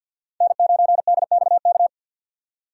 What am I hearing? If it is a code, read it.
Morse code, heard as a keyed tone, about 53 wpm, 700 Hz. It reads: N9DXK
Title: USA Call Signs - 5 Characters - 3x Character Spacing - 50wpm